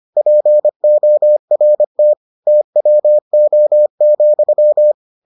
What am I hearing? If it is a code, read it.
Morse code, heard as a keyed tone, about 25 wpm, 600 Hz. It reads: PORT TWO,